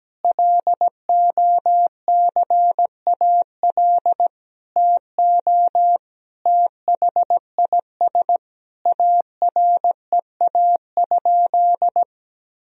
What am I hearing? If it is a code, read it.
Morse code, heard as a keyed tone, about 17 wpm, 700 Hz. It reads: LOCAL TO THIS AREA?
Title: Sentences from Top 1000 Words - Review Part 3 - 17wpm